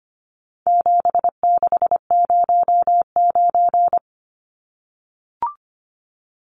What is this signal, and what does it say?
Morse code, heard as a keyed tone, about 25 wpm, 700 Hz. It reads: 7609
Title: Numbers - 4-Digit 25wpm